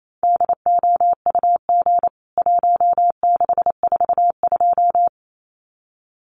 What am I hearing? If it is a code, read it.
Morse code, heard as a keyed tone, about 28 wpm, 700 Hz. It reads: DOUG 1642
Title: CWT Exchange 28wpm